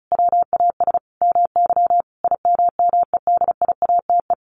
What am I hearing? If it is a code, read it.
Morse code, heard as a keyed tone, about 34 wpm, 700 Hz. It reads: WAS MY IMMEDIATE
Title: Sentences from Top 1500 Words 35wpm